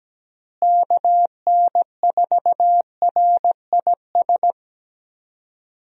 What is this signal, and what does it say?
Morse code, heard as a keyed tone, about 17 wpm, 700 Hz. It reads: KN4RIS
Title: USA Call Signs - 6 Characters - Repeated 3x - 17wpm